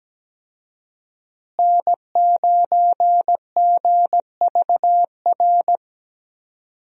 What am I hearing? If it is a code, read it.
Morse code, heard as a keyed tone, about 17 wpm, 700 Hz. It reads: N9GVR
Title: USA Call Signs - 5 Characters - Repeated 2x - 17wpm